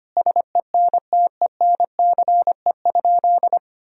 Morse code, heard as a keyed tone, about 25 wpm, 700 Hz. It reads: SENTENCE?
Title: Sentences from Top 1000 Words - Review Part 1 - 25wpm